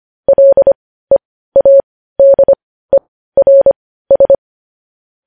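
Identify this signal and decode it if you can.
Morse code, heard as a keyed tone, about 25 wpm, spaced as for 15 wpm, 550 Hz. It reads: LEADERS
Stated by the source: News headlines sent in Morse code at 15 wpm.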